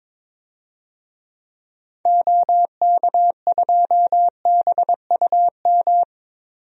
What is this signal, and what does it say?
Morse code, heard as a keyed tone, about 22 wpm, 700 Hz. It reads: OK2BUM
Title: DXCC-100 Call Signs - 6 Characters - 22wpm